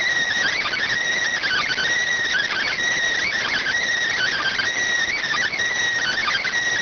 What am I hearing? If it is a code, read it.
Emergency Alert System (SAME) tones (header bursts), heard as harsh buzzing bursts. It is heard from 0.7 to 5.8 s.